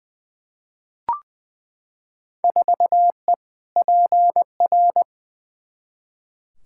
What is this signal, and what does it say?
Morse code, heard as a keyed tone, about 20 wpm, 700 Hz. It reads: 4E PR
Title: Field Day Exchanges Encore - 20wpm